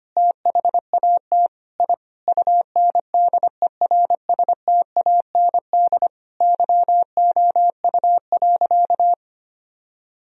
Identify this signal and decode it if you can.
Morse code, heard as a keyed tone, about 25 wpm, 700 Hz. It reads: THAT I UNDERSTAND YOU.
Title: Sentences from Top 1000 Words - Review Part 2 - 25wpm